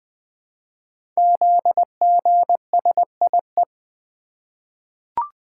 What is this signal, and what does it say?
Morse code, heard as a keyed tone, about 20 wpm, 700 Hz. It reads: ZGSIE
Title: Code Groups - 5 Letters - 20wpm